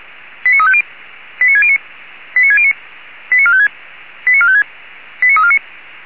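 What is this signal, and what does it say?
Five-tone selective call (ZVEI), heard as quick runs of five tones, tones of about 70 ms. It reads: 89350 89609 89609 89456 89456 89349
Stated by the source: ZVEI five-tone selective call received on SDR by feuerwehr-orlamuende.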